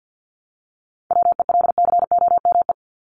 Morse code, heard as a keyed tone, about 48 wpm, 700 Hz. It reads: WELCOME
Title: Top 1751-2000 Words 50wpm